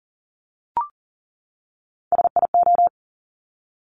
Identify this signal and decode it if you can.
Morse code, heard as a keyed tone, about 38 wpm, 690 Hz. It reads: SIO
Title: Top Three Letter Combinations in Words 40wpm